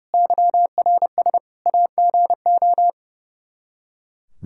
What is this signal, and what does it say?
Morse code, heard as a keyed tone, about 32 wpm, 700 Hz. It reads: YRS AGO
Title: Practice QSO - Exchange 3 - More Info - 30wpm